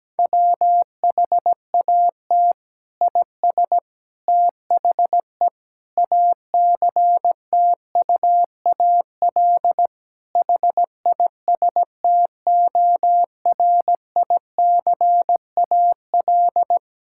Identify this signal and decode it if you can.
Morse code, heard as a keyed tone, about 17 wpm, 700 Hz. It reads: WHAT IS THE ACTUAL HISTORICAL
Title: Sentences from Top 1000 Words - Review Part 5 - 17wpm